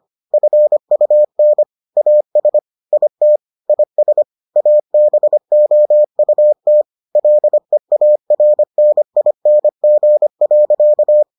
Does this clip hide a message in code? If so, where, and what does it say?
Yes: Morse code from 0.3 to 11.2 s, heard as a keyed tone, about 25 wpm, 600 Hz. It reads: FUN AS IT IS ABOUT LEARNING.